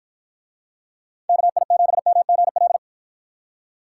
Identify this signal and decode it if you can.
Morse code, heard as a keyed tone, about 53 wpm, 700 Hz. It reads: KI7MGL